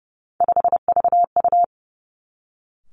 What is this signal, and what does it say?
Morse code, heard as a keyed tone, about 30 wpm, 700 Hz. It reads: 5VU
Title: Three Challenging Letter-Number Combinations 30wpm